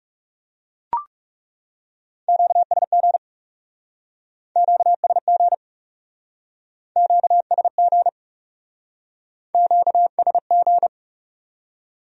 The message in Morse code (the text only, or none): QSG QSG QSG QSG